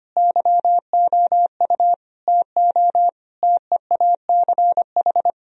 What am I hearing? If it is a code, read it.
Morse code, heard as a keyed tone, about 25 wpm, 700 Hz. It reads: YOU TO TEACH
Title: Sentences from Top 1000 Words - Review Part 2 - 25wpm